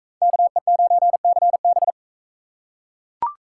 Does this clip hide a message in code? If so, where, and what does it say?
Yes: Morse code from 0.2 to 1.9 s, heard as a keyed tone, about 40 wpm, 700 Hz. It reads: KE9CB